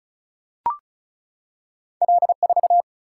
Morse code, heard as a keyed tone, about 36 wpm, 700 Hz. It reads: L4
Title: Single-Letter Single-Number 35wpm